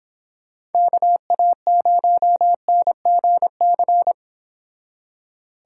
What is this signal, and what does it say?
Morse code, heard as a keyed tone, about 31 wpm, 700 Hz. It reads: KA0NGC